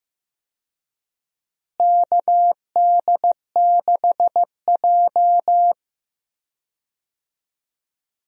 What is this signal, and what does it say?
Morse code, heard as a keyed tone, about 15 wpm, 700 Hz. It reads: KD6J
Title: USA Call Signs - 4 Characters - Repeated 3x - 15wpm